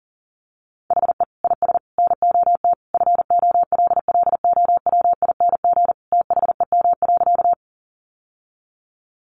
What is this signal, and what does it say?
Morse code, heard as a keyed tone, about 40 wpm, 700 Hz. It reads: HE IS NOT FOLLOWING THEM.
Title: Sentences from Top 700 Words 40wpm